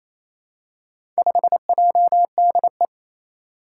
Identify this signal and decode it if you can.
Morse code, heard as a keyed tone, about 27 wpm, 700 Hz. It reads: 5JDE
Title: Four Challenging Letter-Number Combinations 28wpm